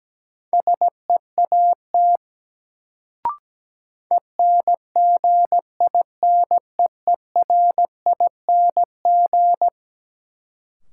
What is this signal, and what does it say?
Morse code, heard as a keyed tone, about 17 wpm, 700 Hz. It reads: SEAT ENGINEERING
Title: Sentences from Top 1500 Words - New Word Drill Alt - 17wpm